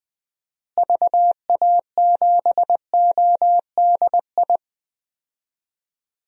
Morse code, heard as a keyed tone, about 20 wpm, 700 Hz. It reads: VA7ODI
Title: DXCC-100 Call Signs - 6 Characters - Repeated 2x - 20wpm